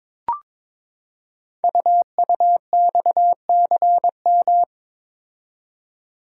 Morse code, heard as a keyed tone, about 22 wpm, 700 Hz. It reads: UUXCM
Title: Code Groups - 5 Letters - 22wpm